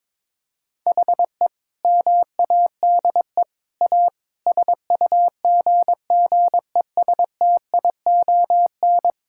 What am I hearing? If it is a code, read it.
Morse code, heard as a keyed tone, about 22 wpm, 700 Hz. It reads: HE MADE A SUGGESTION
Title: Sentences from Top 1600 Words 22wpm